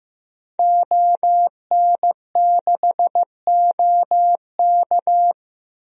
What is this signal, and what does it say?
Morse code, heard as a keyed tone, about 15 wpm, 700 Hz. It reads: ON6OK
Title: DXCC-100 Call Signs - 5 Characters - Repeated 2x - 15wpm